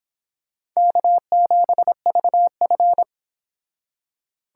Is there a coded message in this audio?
Yes: Morse code from 0.8 to 3.0 s, heard as a keyed tone, about 25 wpm, 700 Hz. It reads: K7VF